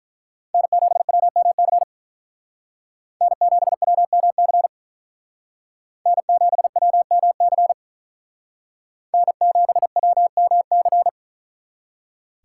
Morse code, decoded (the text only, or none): N7WMC N7WMC N7WMC N7WMC